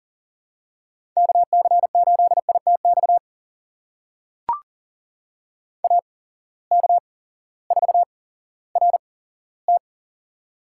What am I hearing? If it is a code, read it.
Morse code, heard as a keyed tone, about 40 wpm, spaced as for 18 wpm, 700 Hz. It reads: KC8ITX AK4RT
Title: USA Call Signs - 8x Character Spacing - 40wpm